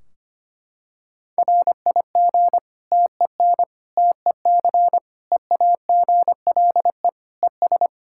Morse code, heard as a keyed tone, about 25 wpm, 700 Hz. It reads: RIG TEN TEC EAGLE ES
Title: K1USN SST QSO Practice 25wpm